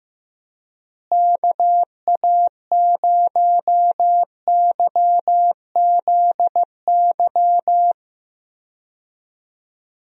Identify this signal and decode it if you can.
Morse code, heard as a keyed tone, about 15 wpm, 700 Hz. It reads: KA0YZY
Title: USA Call Signs - 6 Characters - Repeated 3x - 15wpm